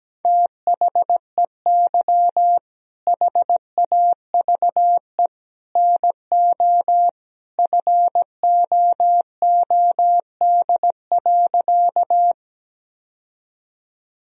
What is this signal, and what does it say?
Morse code, heard as a keyed tone, about 17 wpm, 700 Hz. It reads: THEY HAVE NO FOOD.
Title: Sentences from Top 1000 Words - Review Part 1 - 17wpm